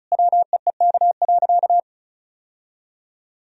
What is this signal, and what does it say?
Morse code, heard as a keyed tone, about 36 wpm, 700 Hz. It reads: WEEK.
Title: Sentences from Top 1000 Words - Review Part 4 - 35wpm